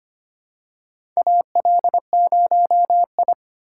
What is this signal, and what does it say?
Morse code, heard as a keyed tone, about 25 wpm, 700 Hz. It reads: AL0I
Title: USA Call Signs - 3x Character Spacing - 25wpm